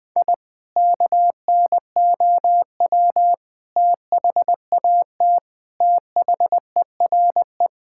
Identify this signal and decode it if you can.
Morse code, heard as a keyed tone, about 20 wpm, 700 Hz. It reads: I KNOW THAT THERE
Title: Sentences with 3-Word Phrases - Spoken Beginning - 20wpm